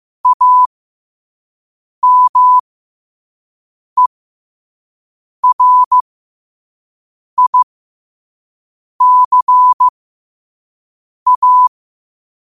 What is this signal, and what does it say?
Morse code, heard as a keyed tone, about 15 wpm, spaced as for 5 wpm, 1000 Hz. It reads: AMERICA
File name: qotdfiles / 5wpm